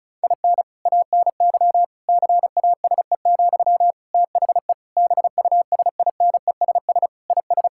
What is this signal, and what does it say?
Morse code, heard as a keyed tone, about 36 wpm, 700 Hz. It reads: IN ANY CASE, THE BUSINESS IS
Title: Sentences from Top 1000 Words - Review Part 5 - 35wpm